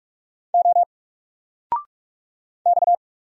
Morse code, heard as a keyed tone, about 44 wpm, 700 Hz. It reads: O X
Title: Single Letter-Number 45wpm